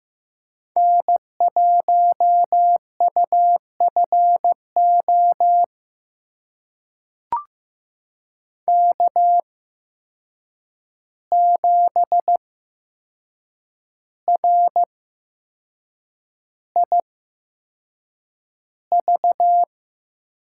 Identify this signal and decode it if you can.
Morse code, heard as a keyed tone, about 15 wpm, spaced as for 6 wpm, 700 Hz. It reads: N1UFO K7RIV
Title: USA Call Signs - 8x Character Spacing - 15wpm